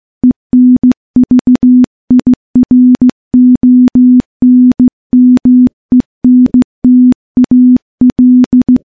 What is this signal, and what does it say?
Morse code, heard as a keyed tone, about 15 wpm, 250 Hz. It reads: ENVIRONMENTAL
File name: morse / morse-training-15WPM